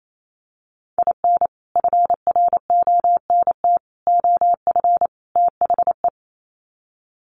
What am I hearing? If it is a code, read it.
Morse code, heard as a keyed tone, about 28 wpm, 700 Hz. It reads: IN FRONT OF THE
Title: Sets of 4 Words 28wpm